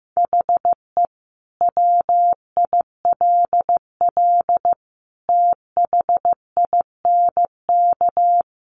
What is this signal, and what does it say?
Morse code, heard as a keyed tone, about 15 wpm, 700 Hz. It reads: HE WILL THINK
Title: Sentences from Top 1100 Words 15wpm